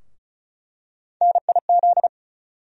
Morse code, heard as a keyed tone, about 36 wpm, 700 Hz. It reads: NIZ
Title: USA Call Sign Postfixes - 3 Letters 35wpm